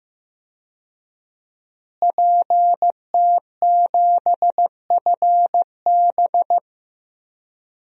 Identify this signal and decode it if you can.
Morse code, heard as a keyed tone, about 15 wpm, 700 Hz. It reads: PT7FB
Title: DXCC-100 Call Signs - 5 Characters - Repeated 2x - 15wpm